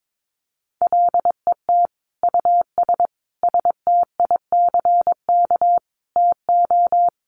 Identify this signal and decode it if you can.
Morse code, heard as a keyed tone, about 22 wpm, 700 Hz. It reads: LET US STICK TO
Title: Sentences from Top 1200 Words 22wpm